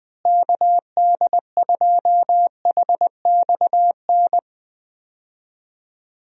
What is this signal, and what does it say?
Morse code, heard as a keyed tone, about 20 wpm, 700 Hz. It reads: KD2HXN
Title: USA Call Signs - 6 Characters - Repeated 3x - 20wpm